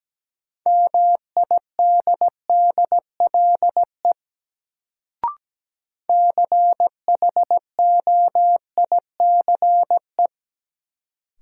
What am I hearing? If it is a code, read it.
Morse code, heard as a keyed tone, about 17 wpm, 700 Hz. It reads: MIDDLE CHOICE
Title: Sentences from Top 900 Words - New Word Drill - 17wpm